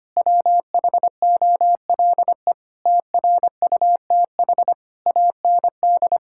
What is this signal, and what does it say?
Morse code, heard as a keyed tone, about 25 wpm, 700 Hz. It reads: WHOLE TRUTH AND
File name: Sets of 9 Words - 25wpm